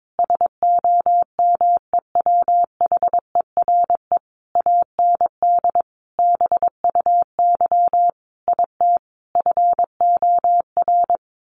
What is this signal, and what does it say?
Morse code, heard as a keyed tone, about 22 wpm, 700 Hz. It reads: SOMEWHERE AND BUY IT FOR
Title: Sentences from Top 1400 Words 22wpm